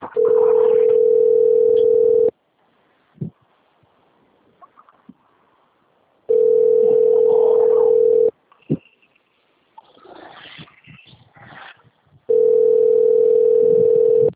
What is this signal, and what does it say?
Ringback tone (North America), 440 + 480 Hz.
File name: rg-1600-unknown- - - .49212